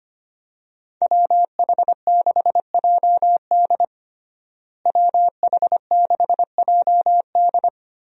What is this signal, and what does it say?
Morse code, heard as a keyed tone, about 25 wpm, 700 Hz. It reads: WH6JD WH6JD
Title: USA Call Signs - 5 Characters - Repeated 3x - 25wpm